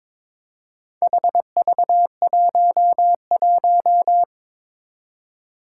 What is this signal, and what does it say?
Morse code, heard as a keyed tone, about 22 wpm, 700 Hz. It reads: HV11
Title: Four Challenging Letter-Number Combinations 22wpm